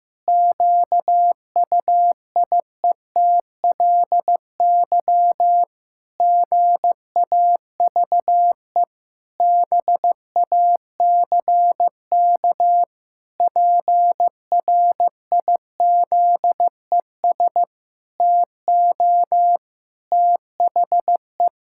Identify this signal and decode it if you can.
Morse code, heard as a keyed tone, about 15 wpm, 700 Hz. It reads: QUIETLY GAVE BACK PRIZES TO THE
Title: Pangram Challenge v2 - 15wpm